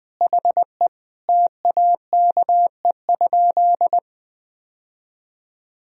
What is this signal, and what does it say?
Morse code, heard as a keyed tone, about 20 wpm, 700 Hz. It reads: HE TAKE?
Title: Sentences from Top 1000 Words - Review Part 1 - 20wpm